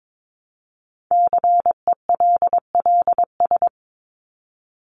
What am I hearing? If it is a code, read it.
Morse code, heard as a keyed tone, about 22 wpm, 700 Hz. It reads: CELLS
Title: ES:S Suffix - Top 100 Words with Base Words 22wpm